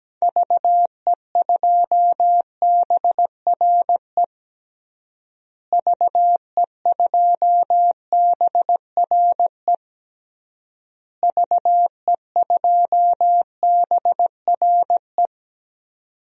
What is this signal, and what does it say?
Morse code, heard as a keyed tone, about 17 wpm, 700 Hz. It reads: VE2BRE VE2BRE VE2BRE
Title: DXCC-100 Call Signs - 6 Characters - Repeated 3x - 17wpm